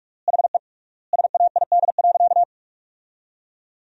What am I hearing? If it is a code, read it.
Morse code, heard as a keyed tone, about 46 wpm, 700 Hz. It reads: HE SAID.
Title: 3x Word-Spacing - Sentences from Top 400 Words - 45wpm